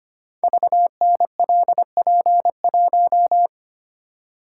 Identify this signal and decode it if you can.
Morse code, heard as a keyed tone, about 26 wpm, 700 Hz. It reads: VNLP1
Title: Code Groups - 5 Letter:Numbers - 25wpm